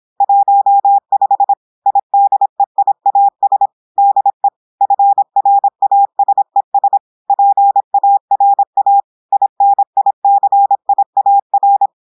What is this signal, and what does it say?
Morse code, heard as a keyed tone, about 26 wpm, 800 Hz. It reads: 15 IDEIAS DE FRASES PARA INICIAR
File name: portugais texte en clair 2 vitesse 26 mots minutes